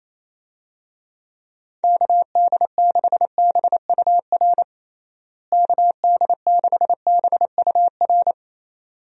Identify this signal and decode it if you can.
Morse code, heard as a keyed tone, about 28 wpm, 700 Hz. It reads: KD6BUR KD6BUR
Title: USA Call Signs - 6 Characters - Repeated 2x - 28wpm